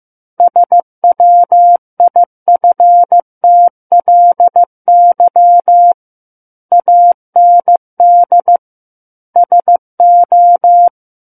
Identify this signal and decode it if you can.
Morse code, heard as a keyed tone, about 15 wpm, 700 Hz. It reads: SWIFTLY AND SO
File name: Book One / 06-Chapter 6 - 15 WPM